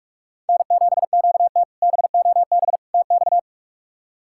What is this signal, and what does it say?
Morse code, heard as a keyed tone, about 46 wpm, 700 Hz. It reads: N7QT BOB TX
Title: K1USN SST QSO Practice - 45wpm